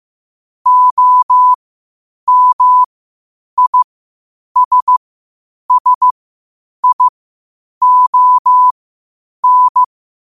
Morse code, heard as a keyed tone, about 15 wpm, spaced as for 8 wpm, 1000 Hz. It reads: OMISSION
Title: 8 WPM morse code quotes for Thu, 29 Feb 2024